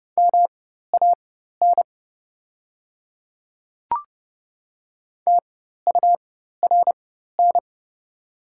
Morse code, heard as a keyed tone, about 30 wpm, spaced as for 14 wpm, 700 Hz. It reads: MAN TURN
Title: 4x Character Spacing - Top 200 Words - 30wpm